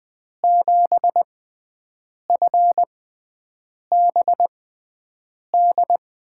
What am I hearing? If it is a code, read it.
Morse code, heard as a keyed tone, about 20 wpm, spaced as for 7 wpm, 700 Hz. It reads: 7FBD
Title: USA Call Signs - 5 Characters - 6x Character Spacing - 20wpm